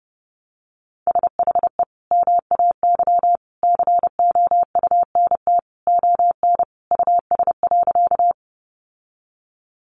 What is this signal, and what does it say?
Morse code, heard as a keyed tone, about 30 wpm, 700 Hz. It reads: SHE MAY COUNT ON US.
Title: Sentences from Top 1500 Words 30wpm